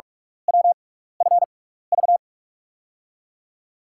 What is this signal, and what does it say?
Morse code, heard as a keyed tone, about 44 wpm, spaced as for 15 wpm, 690 Hz. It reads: WFV
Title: USA Call Signs - 5 Characters - 6x Character Spacing - 45wpm